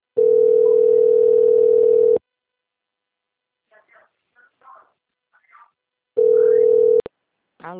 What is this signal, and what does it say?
Ringback tone (North America), 440 + 480 Hz.
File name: out- - - - - .16009